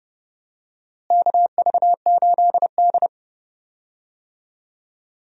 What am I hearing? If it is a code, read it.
Morse code, heard as a keyed tone, about 30 wpm, 700 Hz. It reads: KV8D
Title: USA Call Signs - 4 Characters - Repeated 2x - 30wpm